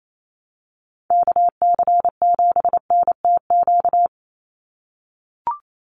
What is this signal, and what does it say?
Morse code, heard as a keyed tone, about 28 wpm, 700 Hz. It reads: KC7NTQ
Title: USA Call Signs 28wpm